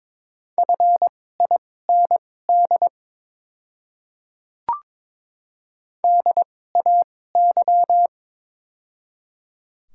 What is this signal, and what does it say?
Morse code, heard as a keyed tone, about 22 wpm, spaced as for 16 wpm, 700 Hz. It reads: FIND DAY
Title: 2x Character Spacing - Top 200 Words - 22wpm